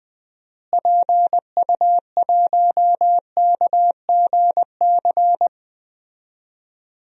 Morse code, heard as a keyed tone, about 20 wpm, 700 Hz. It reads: PU1KGC
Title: DXCC-100 Call Signs - 6 Characters - Repeated 3x - 20wpm